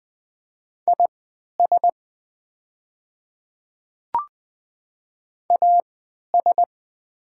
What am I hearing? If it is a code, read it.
Morse code, heard as a keyed tone, about 20 wpm, spaced as for 11 wpm, 700 Hz. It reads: IS AS
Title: 3x Character Spacing - Top 100 Words - 20wpm